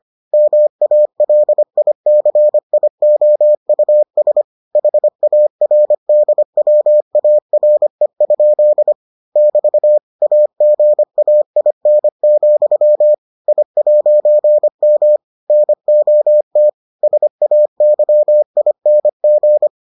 Morse code, heard as a keyed tone, about 25 wpm, 600 Hz. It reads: MALICIOUS HARDWARE? <BT> AGAIN, I'M NOT SAYING